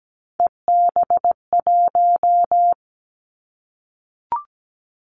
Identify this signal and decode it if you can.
Morse code, heard as a keyed tone, about 17 wpm, 700 Hz. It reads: EB1
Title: Callsign Prefixes for Top 25 DXCC - 17wpm